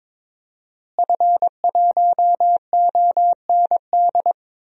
Morse code, heard as a keyed tone, about 22 wpm, 700 Hz. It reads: F1OND
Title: DXCC-100 Call Signs - 5 Characters - Repeated 2x - 22wpm